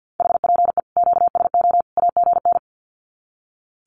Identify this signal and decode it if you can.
Morse code, heard as a keyed tone, about 50 wpm, 700 Hz. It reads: HPE QSO AGN
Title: Common QSO Phrases 50wpm